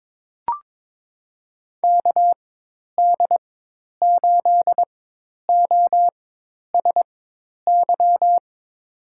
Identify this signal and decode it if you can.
Morse code, heard as a keyed tone, about 22 wpm, spaced as for 10 wpm, 700 Hz. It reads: KD8OSY
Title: USA Call Signs - 6 Characters - 4x Character Spacing - 22wpm